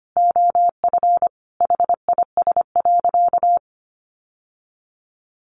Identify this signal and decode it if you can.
Morse code, heard as a keyed tone, about 25 wpm, 700 Hz. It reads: OF HIS.
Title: Sentences from Top 1000 Words 25wpm